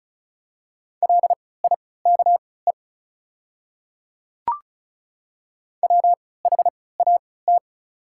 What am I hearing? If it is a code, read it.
Morse code, heard as a keyed tone, about 36 wpm, spaced as for 20 wpm, 700 Hz. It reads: LIKE WHAT
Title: 3x Character Spacing - Top 200 Words - 35wpm